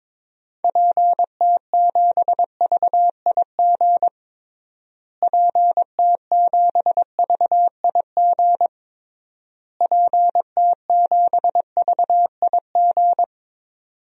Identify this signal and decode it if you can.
Morse code, heard as a keyed tone, about 22 wpm, 700 Hz. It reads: PT7VIG PT7VIG PT7VIG
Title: DXCC-100 Call Signs - 6 Characters - Repeated 3x - 22wpm